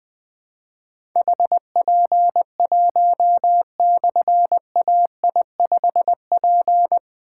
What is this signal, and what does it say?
Morse code, heard as a keyed tone, about 20 wpm, 700 Hz. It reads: HP1/AI5P
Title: Difficult Call Signs 20wpm